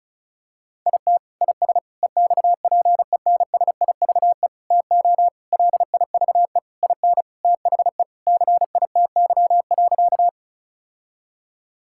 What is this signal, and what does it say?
Morse code, heard as a keyed tone, about 36 wpm, 700 Hz. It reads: IT IS EXPENSIVE TO LIVE IN THE CITY.
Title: Sentences from Top 1000 Words - Review Part 6 - 35wpm